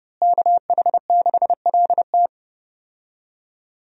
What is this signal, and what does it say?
Morse code, heard as a keyed tone, about 30 wpm, 700 Hz. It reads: KH6LT
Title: USA Call Signs - 5 Characters - Repeated 2x - 30wpm